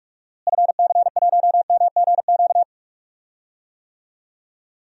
Morse code, heard as a keyed tone, about 41 wpm, 700 Hz. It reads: FC1MGQ
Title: DXCC-100 Call Signs - 6 Characters - Repeated 2x - 45wpm